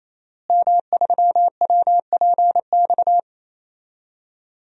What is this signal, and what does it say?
Morse code, heard as a keyed tone, about 28 wpm, 700 Hz. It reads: M3WPX
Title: DXCC-100 Call Signs - 5 Characters - Repeated 3x - 28wpm